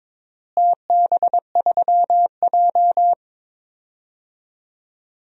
Morse code, heard as a keyed tone, about 22 wpm, 700 Hz. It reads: TB3J
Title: DXCC-100 Call Signs - 4 Characters - Repeated 2x - 22wpm